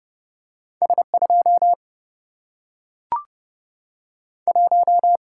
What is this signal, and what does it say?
Morse code, heard as a keyed tone, about 30 wpm, 700 Hz. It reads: S2 1
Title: Single-Letter Single-Number 30wpm